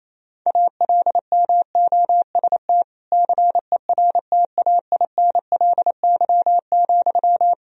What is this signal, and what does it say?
Morse code, heard as a keyed tone, about 28 wpm, 700 Hz. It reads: ALMOST CERTAINLY,
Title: Sentences from Top 1000 Words - Review Part 4 - 28wpm